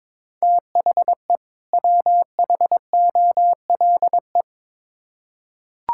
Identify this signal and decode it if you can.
Morse code, heard as a keyed tone, about 22 wpm, 700 Hz. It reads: THE WHOLE
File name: Build A Phrase - 2-Word Phrases - 22wpm